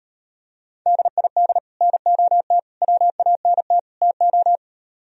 Morse code, heard as a keyed tone, about 39 wpm, 700 Hz. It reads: DID NOT WANT TO